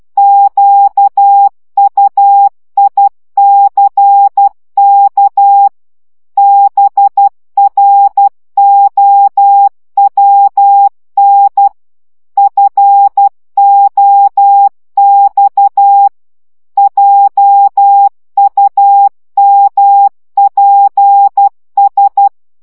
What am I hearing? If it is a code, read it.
Morse code, heard as a keyed tone, about 12 wpm, 800 Hz. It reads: QUICK BROWN FOX JUMPS
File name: CW 10 baud